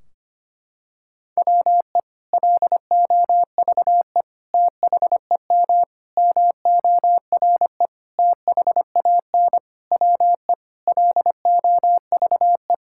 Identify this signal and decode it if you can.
Morse code, heard as a keyed tone, about 25 wpm, 700 Hz. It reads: WE LOVE THEM MORE THAN WE LOVE
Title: Sentences from Top 1600 Words 25wpm